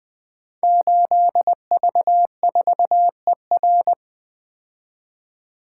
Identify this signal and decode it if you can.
Morse code, heard as a keyed tone, about 20 wpm, 700 Hz. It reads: 8V4ER
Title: Code Groups - 5 Letter:Numbers - 20wpm